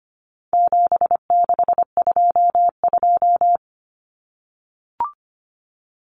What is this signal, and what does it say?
Morse code, heard as a keyed tone, about 25 wpm, 700 Hz. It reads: 7622
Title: Common Cut Numbers - 4-Digit 25wpm